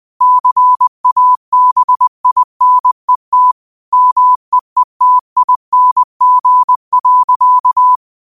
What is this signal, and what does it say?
Morse code, heard as a keyed tone, about 20 wpm, 1000 Hz. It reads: CABINET MEETING.